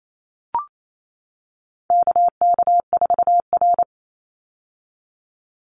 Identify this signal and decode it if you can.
Morse code, heard as a keyed tone, about 28 wpm, 700 Hz. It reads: KK4R
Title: Call Signs - 28wpm